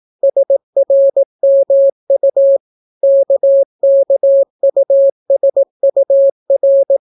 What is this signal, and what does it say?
Morse code, heard as a keyed tone, about 18 wpm, 550 Hz. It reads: SRMU KKUSUR